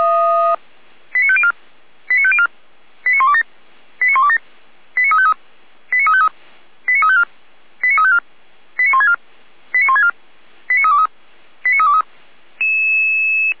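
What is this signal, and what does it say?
Five-tone selective call (ZVEI), heard as quick runs of five tones, tones of about 70 ms. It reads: 89504 89504 89127 89127 89353 89353 89354 89354 89164 89164 89323 89323
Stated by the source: ZVEI five-tone selective call received with an SDR.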